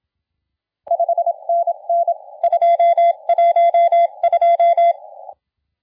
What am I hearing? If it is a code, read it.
Morse code, heard as a keyed tone, about 25 wpm, 680 Hz. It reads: 5NN 212
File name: la4c / sac-cw-2017 / CONTEST AUDIO 1100Z